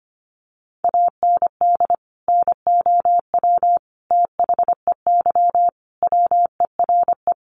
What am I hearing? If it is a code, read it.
Morse code, heard as a keyed tone, about 25 wpm, 700 Hz. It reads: AND NOW THEY WERE
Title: Sentences from Top 1100 Words 25wpm